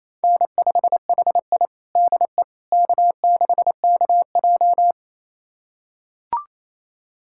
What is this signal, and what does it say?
Morse code, heard as a keyed tone, about 28 wpm, 700 Hz. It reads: N5HI DE K6KJ
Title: USA Callsign DE Callsign - 4 Characters - 28wpm